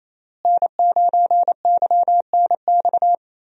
Morse code, heard as a keyed tone, about 28 wpm, 700 Hz. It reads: N9YNX
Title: USA Call Signs - 5 Characters - Repeated 3x - 28wpm